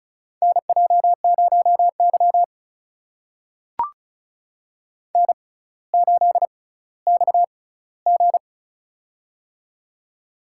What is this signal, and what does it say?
Morse code, heard as a keyed tone, about 35 wpm, spaced as for 18 wpm, 700 Hz. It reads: NJ0Y N8XG
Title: USA Call Signs - 4 Characters - 6x Character Spacing - 35wpm